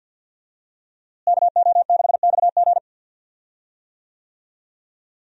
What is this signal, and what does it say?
Morse code, heard as a keyed tone, about 52 wpm, 700 Hz. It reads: KO6XG
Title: USA Call Signs - 5 Characters - Repeated 3x - 50wpm